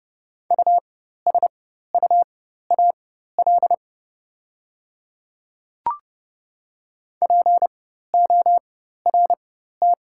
Morse code, heard as a keyed tone, about 30 wpm, spaced as for 14 wpm, 700 Hz. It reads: USUAL PORT
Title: 4x Character Spacing - Top 500 Words - 30wpm